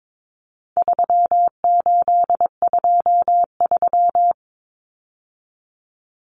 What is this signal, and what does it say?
Morse code, heard as a keyed tone, about 22 wpm, 700 Hz. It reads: 3823
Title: Common Cut Numbers - 4-Digit 22wpm